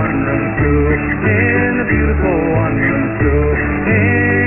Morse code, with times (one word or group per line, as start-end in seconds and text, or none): none